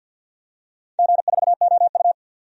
Morse code, heard as a keyed tone, about 53 wpm, 700 Hz. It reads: G4OU
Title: DXCC-100 Call Signs - 4 Characters - 50wpm